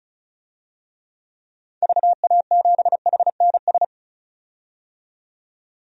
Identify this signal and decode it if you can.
Morse code, heard as a keyed tone, about 34 wpm, 700 Hz. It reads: VA7HNS
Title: DXCC-100 Call Signs - 6 Characters - 35wpm